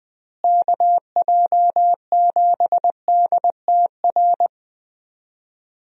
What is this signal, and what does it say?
Morse code, heard as a keyed tone, about 21 wpm, 700 Hz. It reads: KJ7DTR
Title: USA Call Signs - 6 Characters - Repeated 3x - 20wpm